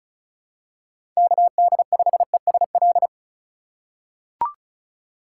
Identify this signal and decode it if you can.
Morse code, heard as a keyed tone, about 36 wpm, 700 Hz. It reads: KD5ESL
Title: USA Call Signs - 3x Character Spacing - 35wpm